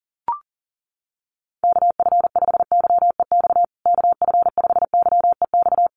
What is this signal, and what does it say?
Morse code, heard as a keyed tone, about 40 wpm, 700 Hz. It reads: KF5YEX KF5YEX
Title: USA Call Signs Repeated 3x 40wpm